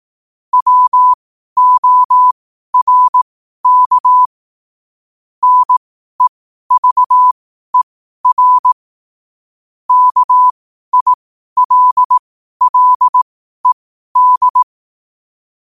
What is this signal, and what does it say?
Morse code, heard as a keyed tone, about 18 wpm, spaced as for 12 wpm, 1000 Hz. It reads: WORK NEVER KILLED
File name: qotdfiles / 12wpm